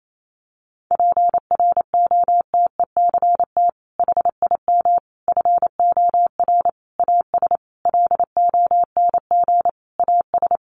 Morse code, heard as a keyed tone, about 28 wpm, 700 Hz. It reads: PROTECT HIM FOR AS LONG AS
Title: Sentences from Top 900 Words 28wpm